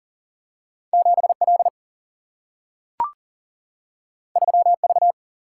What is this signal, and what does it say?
Morse code, heard as a keyed tone, about 39 wpm, 700 Hz. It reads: 7 L 3 V
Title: Single-Letter Single-Number 40wpm